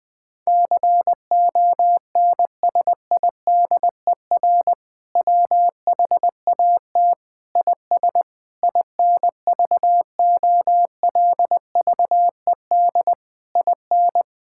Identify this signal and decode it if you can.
Morse code, heard as a keyed tone, about 20 wpm, 700 Hz. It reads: CONSIDER WHAT IS INVOLVED IN
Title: Sentences from Top 1000 Words - Review Part 5 - 20wpm